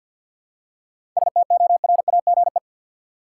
Morse code, heard as a keyed tone, about 53 wpm, 700 Hz. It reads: STORAGE
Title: Seven Letter Words - 50wpm